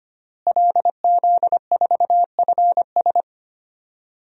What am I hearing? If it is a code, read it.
Morse code, heard as a keyed tone, about 25 wpm, 700 Hz. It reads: LZ4FS
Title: DXCC-100 Call Signs - 5 Characters - Repeated 3x - 25wpm